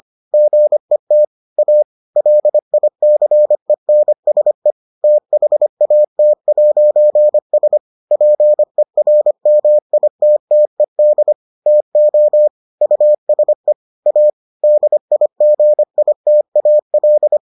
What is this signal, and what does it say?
Morse code, heard as a keyed tone, about 25 wpm, 600 Hz. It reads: GET A LICENSE THAT'S PERMITTED TO USE A DIGITAL